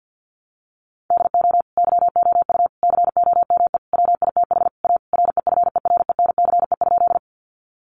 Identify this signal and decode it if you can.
Morse code, heard as a keyed tone, about 50 wpm, 700 Hz. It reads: DO YOU COME WITH A REFERENCE?
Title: Sentences from Top 1200 Words 50wpm